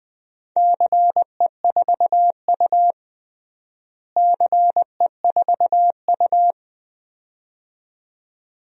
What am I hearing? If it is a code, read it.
Morse code, heard as a keyed tone, about 20 wpm, 700 Hz. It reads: CE4U CE4U
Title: DXCC-100 Call Signs - 4 Characters - Repeated 2x - 20wpm